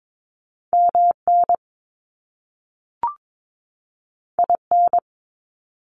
Morse code, heard as a keyed tone, about 22 wpm, 700 Hz. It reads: MN IN
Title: US States 22wpm